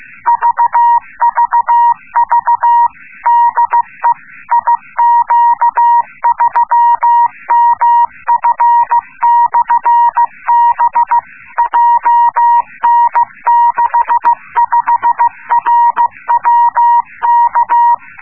Morse code, heard as a keyed tone, about 15 wpm, 950 Hz. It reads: VVV DE IQ3MF/B JN65RWK